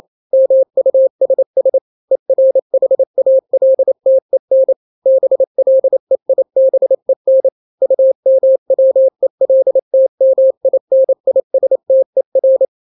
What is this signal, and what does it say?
Morse code, heard as a keyed tone, about 28 wpm, 530 Hz. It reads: MUSS ERHALTEN BLEIBEN UMWELTMINISTER
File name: daily / 27wpm